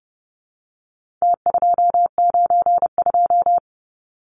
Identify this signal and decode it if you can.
Morse code, heard as a keyed tone, about 30 wpm, 700 Hz. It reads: T292
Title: Numbers - Contest Style - 4-Digit 30wpm